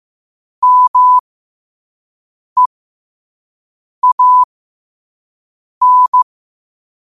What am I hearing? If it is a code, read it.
Morse code, heard as a keyed tone, about 15 wpm, spaced as for 5 wpm, 1000 Hz. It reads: MEAN
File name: qotdfiles / 5wpm